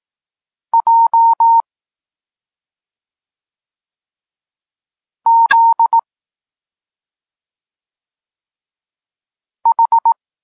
Morse code, heard as a keyed tone, about 18 wpm, 920 Hz. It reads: J Z H